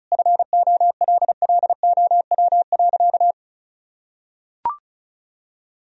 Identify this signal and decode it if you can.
Morse code, heard as a keyed tone, about 34 wpm, 700 Hz. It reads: FOLLOW.
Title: Sentences from Top 1000 Words - Review Part 2 - 35wpm